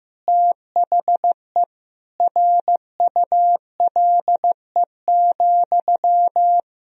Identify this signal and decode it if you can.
Morse code, heard as a keyed tone, about 15 wpm, 700 Hz. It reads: THE RULE,
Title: Sentences from Top 400 Words and Abbreviations 15wpm